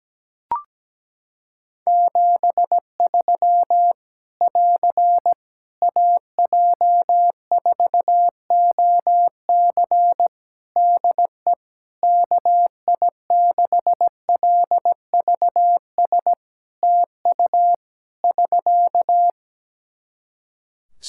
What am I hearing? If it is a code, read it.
Morse code, heard as a keyed tone, about 17 wpm, 700 Hz. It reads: 73 <AR> AJ4OC DE KI6LVS TU <SK>
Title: Practice QSO - Exchange 4 - Ending - 17wpm